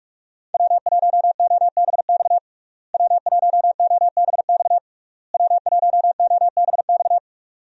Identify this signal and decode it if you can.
Morse code, heard as a keyed tone, about 45 wpm, 700 Hz. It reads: W1OBX W1OBX W1OBX
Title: USA Call Signs - 5 Characters - Repeated 3x - 45wpm